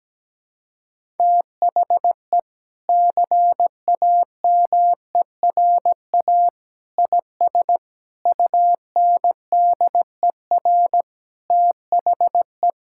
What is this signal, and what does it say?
Morse code, heard as a keyed tone, about 17 wpm, 700 Hz. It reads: THE CAMERA IS UNDER THE
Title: Sentences from Top 1000 Words - Review Part 5 - 17wpm